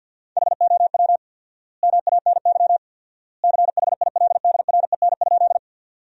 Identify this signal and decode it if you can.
Morse code, heard as a keyed tone, about 50 wpm, 700 Hz. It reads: HOW MANY CHILDREN?
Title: 4x Word-Spacing - Sentences from Top 300 Words - 50wpm